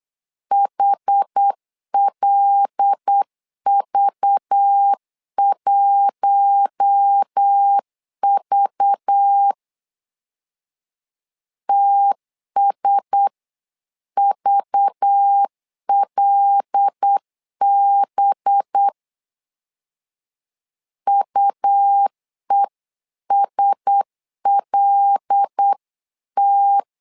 Morse code, heard as a keyed tone, about 8 wpm, 800 Hz. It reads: HLV1V TSVLB UESLT
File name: LEZIONI / 5 LBVC / 5 LBV 8wpm 2837